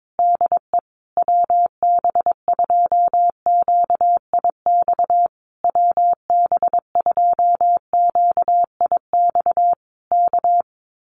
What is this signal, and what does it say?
Morse code, heard as a keyed tone, about 22 wpm, 700 Hz. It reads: DE WB2QIX WB2QIX K
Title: Common QSO Phrases Encore 22wpm